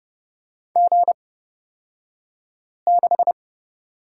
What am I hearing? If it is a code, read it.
Morse code, heard as a keyed tone, about 29 wpm, spaced as for 5 wpm, 700 Hz. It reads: G6
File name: Lesson 36d - BT - Two Characters 30wpm